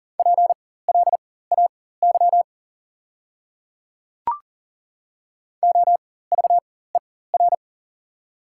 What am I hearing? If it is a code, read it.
Morse code, heard as a keyed tone, about 40 wpm, spaced as for 19 wpm, 690 Hz. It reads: PLAY OVER
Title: 4x Character Spacing - Top 300 Words - 40wpm